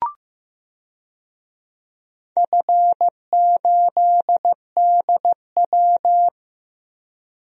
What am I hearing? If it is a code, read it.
Morse code, heard as a keyed tone, about 15 wpm, 700 Hz. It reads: F8DW
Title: DXCC-100 Call Signs - 4 Characters - 15wpm